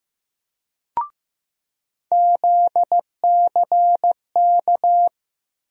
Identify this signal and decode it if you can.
Morse code, heard as a keyed tone, about 16 wpm, 700 Hz. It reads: ZCK
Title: USA Call Sign Postfixes - 3 Letters 15wpm